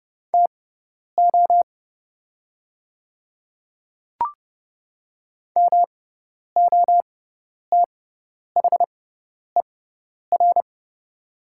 Morse code, heard as a keyed tone, about 30 wpm, spaced as for 10 wpm, 690 Hz. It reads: TO MOTHER
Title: 6x Character Spacing - Top 200 Words - 30wpm